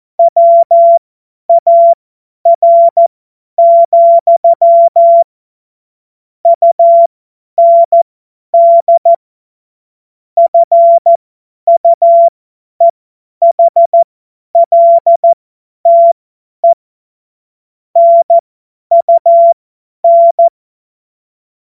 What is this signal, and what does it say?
Morse code, heard as a keyed tone, about 14 wpm, spaced as for 10 wpm, 680 Hz. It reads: WAR, UND FUEHLTE NUN
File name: Morsekurs / Witze / Witz 25. Oktober 2022 50 F70